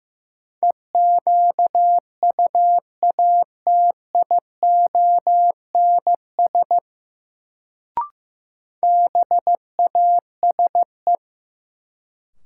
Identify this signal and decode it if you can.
Morse code, heard as a keyed tone, about 16 wpm, 700 Hz. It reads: EQUATIONS BASE
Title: Sets of 5 Words - New Word Drill - 15wpm